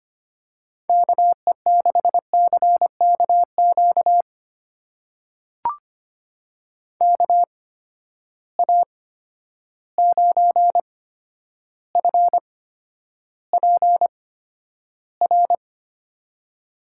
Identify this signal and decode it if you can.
Morse code, heard as a keyed tone, about 25 wpm, spaced as for 11 wpm, 700 Hz. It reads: KE6CKQ KA9FPR
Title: USA Call Signs - 6 Characters - 8x Character Spacing - 25wpm